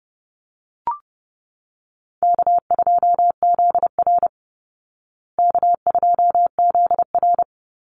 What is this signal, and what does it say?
Morse code, heard as a keyed tone, about 30 wpm, 700 Hz. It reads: K2ZR K2ZR
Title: Call Signs Repeated 2x - 30wpm